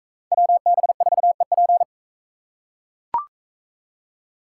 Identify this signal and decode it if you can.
Morse code, heard as a keyed tone, about 40 wpm, 700 Hz. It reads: WB4EP